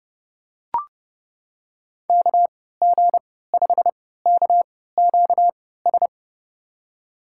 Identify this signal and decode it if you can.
Morse code, heard as a keyed tone, about 30 wpm, spaced as for 17 wpm, 700 Hz. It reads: KG5KQS
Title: USA Call Signs - 6 Characters - 3x Character Spacing - 30wpm